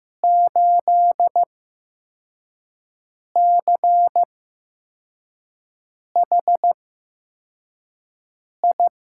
Morse code, heard as a keyed tone, about 15 wpm, spaced as for 4 wpm, 700 Hz. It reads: 8CHI
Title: USA Call Signs - 6 Characters - 8x Character Spacing - 15wpm